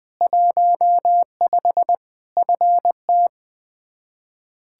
Morse code, heard as a keyed tone, about 20 wpm, 700 Hz. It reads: 15 FT
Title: Practice QSO - Exchange 2 - Info - 20wpm